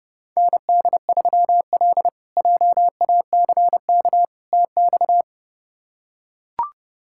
Morse code, heard as a keyed tone, about 32 wpm, 700 Hz. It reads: ND3L JACK TX
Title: K1USN SST QSO Practice - 30wpm